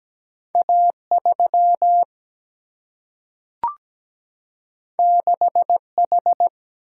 Morse code, heard as a keyed tone, about 17 wpm, 700 Hz. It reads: A 3 6 H
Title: Single-Letter Single-Number 17wpm